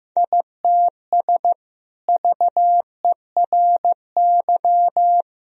Morse code, heard as a keyed tone, about 15 wpm, 700 Hz. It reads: ITS VERY
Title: Sentences from Top 1000 Words - Review Part 2 - 15wpm